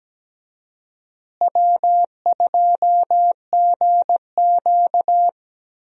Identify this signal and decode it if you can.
Morse code, heard as a keyed tone, about 17 wpm, 700 Hz. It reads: W2GQ
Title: USA Call Signs - 4 Characters - Repeated 3x - 17wpm